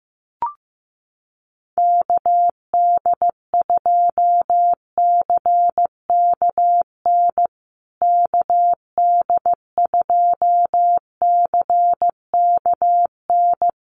Morse code, heard as a keyed tone, about 15 wpm, 700 Hz. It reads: KD2CKN KD2CKN
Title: USA Call Signs Repeated 2x 15wpm